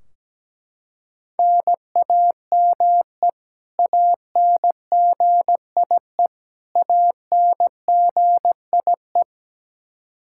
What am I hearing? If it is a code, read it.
Morse code, heard as a keyed tone, about 17 wpm, 700 Hz. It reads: NAME ANGIE ANGIE
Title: Practice QSO - Exchange 1 - Beginning - 17wpm